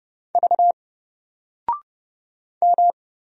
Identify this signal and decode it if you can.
Morse code, heard as a keyed tone, about 30 wpm, 700 Hz. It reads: V M
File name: Single-Letter-Number-30wpm